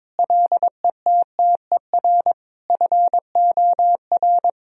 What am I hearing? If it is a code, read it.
Morse code, heard as a keyed tone, about 22 wpm, 700 Hz. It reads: LETTER FOR
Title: Sentences from Top 1000 Words - Review Part 2 - 22wpm